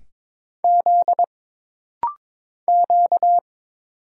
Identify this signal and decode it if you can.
Morse code, heard as a keyed tone, about 22 wpm, 700 Hz. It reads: Z Q
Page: This rapid-fire timing is geared to push your ICR (Instant Character Recognition) skills.